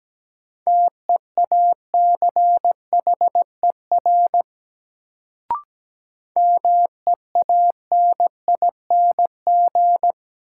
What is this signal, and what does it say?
Morse code, heard as a keyed tone, about 17 wpm, 700 Hz. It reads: TEACHER MEANING
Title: Sentences from Top 900 Words - New Word Drill - 17wpm